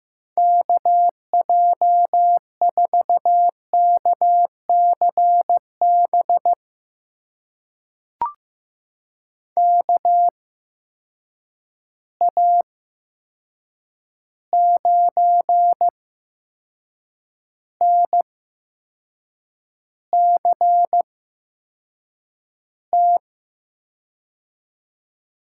Morse code, heard as a keyed tone, about 15 wpm, spaced as for 6 wpm, 700 Hz. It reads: KJ4KCB KA9NCT
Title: USA Call Signs - 6 Characters - 8x Character Spacing - 15wpm